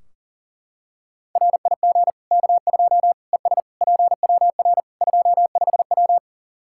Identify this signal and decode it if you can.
Morse code, heard as a keyed tone, about 40 wpm, 700 Hz. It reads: RIG K2 ES PWR 25W
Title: Practice QSO - Exchange 2 - Info - 40wpm